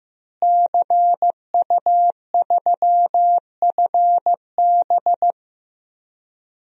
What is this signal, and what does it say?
Morse code, heard as a keyed tone, about 15 wpm, 700 Hz. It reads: CU3FB
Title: DXCC-100 Call Signs - 5 Characters - Repeated 2x - 15wpm